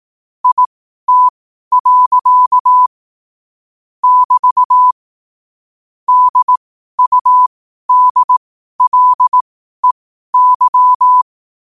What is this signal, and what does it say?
Morse code, heard as a keyed tone, about 18 wpm, spaced as for 12 wpm, 1000 Hz. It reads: IT. <BT> DUDLEY
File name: qotdfiles / 12wpm